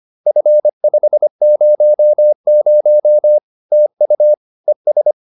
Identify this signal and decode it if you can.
Morse code, heard as a keyed tone, about 25 wpm, 600 Hz. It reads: F500 TU ES